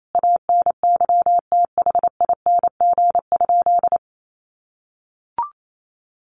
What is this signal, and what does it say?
Morse code, heard as a keyed tone, about 28 wpm, 700 Hz. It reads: ANYTHING?
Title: Sentences from Top 1200 Words 28wpm